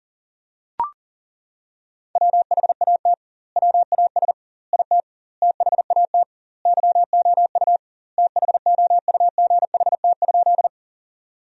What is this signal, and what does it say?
Morse code, heard as a keyed tone, about 40 wpm, 700 Hz. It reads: WHAT WAS IT THAT YOU THOUGHT?
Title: 2x Word-Spacing - Sentences from Top 300 Words - 40wpm